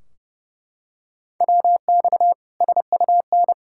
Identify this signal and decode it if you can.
Morse code, heard as a keyed tone, about 30 wpm, 700 Hz. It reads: WX SUN
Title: Practice QSO - Exchange 2 - Info - 30wpm